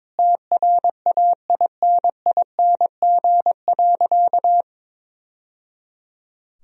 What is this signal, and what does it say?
Morse code, heard as a keyed tone, about 22 wpm, 700 Hz. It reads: TRAINING.
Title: Sentences from Top 1000 Words - Review Part 4 - 22wpm